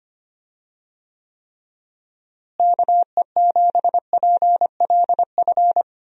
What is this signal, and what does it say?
Morse code, heard as a keyed tone, about 25 wpm, 700 Hz. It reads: KE7PLF
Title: USA Call Signs - 6 Characters - Repeated 3x - 25wpm